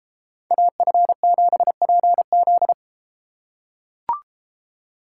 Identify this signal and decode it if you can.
Morse code, heard as a keyed tone, about 33 wpm, 700 Hz. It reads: AF7PZ